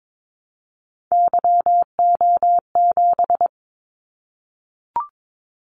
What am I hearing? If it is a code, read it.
Morse code, heard as a keyed tone, about 22 wpm, 700 Hz. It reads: YO7
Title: Callsign Prefixes for Top 25 DXCC - 22wpm